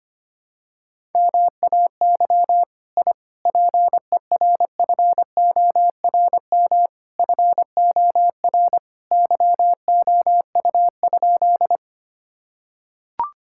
Morse code, heard as a keyed tone, about 25 wpm, 700 Hz. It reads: MAY I PERFORM FOR YOU?
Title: Sentences from Top 1000 Words - Review Part 5 - 25wpm